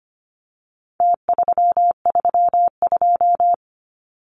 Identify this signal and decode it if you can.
Morse code, heard as a keyed tone, about 25 wpm, 700 Hz. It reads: T332
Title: Numbers - Contest Style - 4-Digit 25wpm